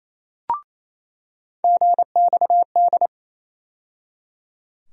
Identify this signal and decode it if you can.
Morse code, heard as a keyed tone, about 28 wpm, 700 Hz. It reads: GXD
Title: USA Call Sign Postfixes - 3 Letters 28wpm